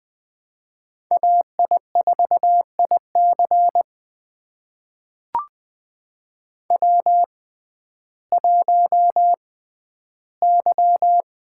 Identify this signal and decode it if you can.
Morse code, heard as a keyed tone, about 20 wpm, spaced as for 12 wpm, 700 Hz. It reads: AI4IC W1Y
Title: USA Call Signs - 5 Characters - 6x Character Spacing - 20wpm